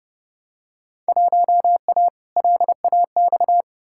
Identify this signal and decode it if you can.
Morse code, heard as a keyed tone, about 31 wpm, 700 Hz. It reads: 1A LAX
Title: Field Day Exchanges - 30wpm